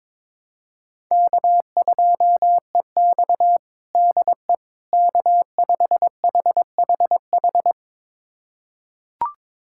Morse code, heard as a keyed tone, about 22 wpm, 700 Hz. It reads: K2EX DE K5HHH
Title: USA Callsign DE Callsign 22wpm